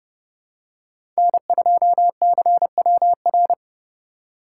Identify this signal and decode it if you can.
Morse code, heard as a keyed tone, about 29 wpm, 700 Hz. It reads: N2CWR